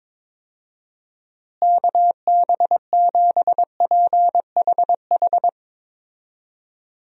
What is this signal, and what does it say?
Morse code, heard as a keyed tone, about 22 wpm, 700 Hz. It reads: KB7PHH
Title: USA Call Signs - 6 Characters - Repeated 3x - 22wpm